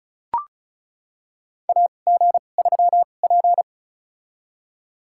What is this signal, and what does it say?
Morse code, heard as a keyed tone, about 36 wpm, spaced as for 25 wpm, 700 Hz. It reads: AG3P
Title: USA Call Signs - 4 Characters - 2x Character Spacing - 35wpm